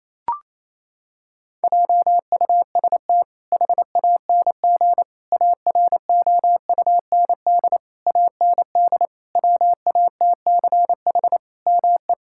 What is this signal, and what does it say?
Morse code, heard as a keyed tone, about 27 wpm, 700 Hz. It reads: JUST HANG AROUND AND WATCH ME
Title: Sentences from Top 1600 Words 28wpm